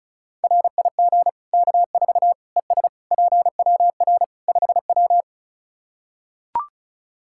Morse code, heard as a keyed tone, about 36 wpm, 700 Hz. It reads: RIG K4 ES PWR 5W
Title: K1USN SST QSO Practice 35wpm